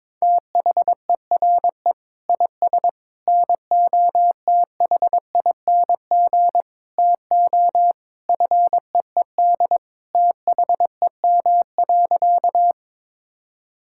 Morse code, heard as a keyed tone, about 22 wpm, 700 Hz. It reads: THERE IS NOTHING TO FEED THEM.
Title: Sentences from Top 1600 Words 22wpm